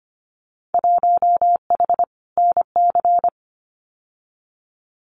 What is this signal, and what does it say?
Morse code, heard as a keyed tone, about 25 wpm, 700 Hz. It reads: 1H NC
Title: Winter Field Day Exchanges 25wpm